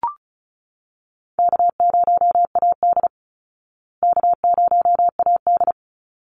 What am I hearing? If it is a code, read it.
Morse code, heard as a keyed tone, about 34 wpm, 700 Hz. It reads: K0AD K0AD
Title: Call Signs Repeated 3x - 35wpm